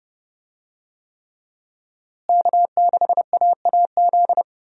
Morse code, heard as a keyed tone, about 30 wpm, 700 Hz. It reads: K6AAZ
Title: USA Call Signs - 5 Characters - Repeated 3x - 30wpm